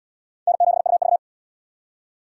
Morse code, heard as a keyed tone, about 68 wpm, 700 Hz. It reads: N7UV